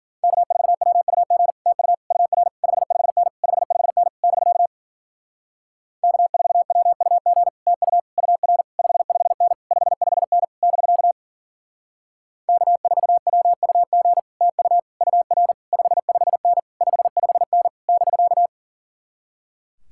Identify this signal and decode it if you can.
Morse code, heard as a keyed tone, about 49 wpm, 700 Hz. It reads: K4WUG TU UR 55N 55N <BK> K4WUG TU UR 55N 55N <BK> K4WUG TU UR 55N 55N <BK>
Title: Speed-Racing Common QSO Phrases - 40wpm